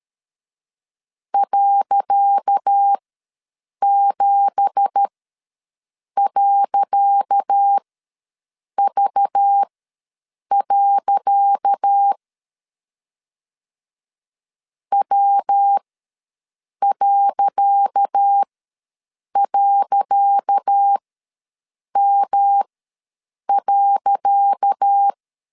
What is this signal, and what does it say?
Morse code, heard as a keyed tone, about 13 wpm, spaced as for 7 wpm, 800 Hz. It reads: .7.V. W..M.
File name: LEZIONI / 10 SIGNS / 10C DOT ws 12wpm 3548